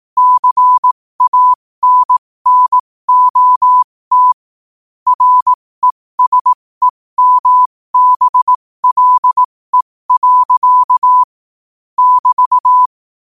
Morse code, heard as a keyed tone, about 18 wpm, spaced as for 15 wpm, 1000 Hz. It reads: CANNOT RESEMBLE. <BT>
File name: qotdfiles / 15wpm